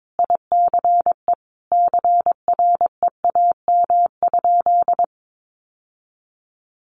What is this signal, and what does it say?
Morse code, heard as a keyed tone, about 22 wpm, 700 Hz. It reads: ICE CREAM?
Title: Sentences from Top 1500 Words 22wpm